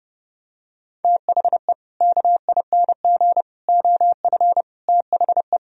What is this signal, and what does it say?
Morse code, heard as a keyed tone, about 30 wpm, 700 Hz. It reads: THE KING OF THE
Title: Sentences from Top 1000 Words - Review Part 2 - 30wpm